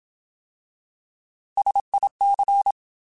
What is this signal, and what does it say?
Morse code, heard as a keyed tone, about 27 wpm, 800 Hz. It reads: SIC
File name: Alphabetical / Alphabetical mp3s / Ss @25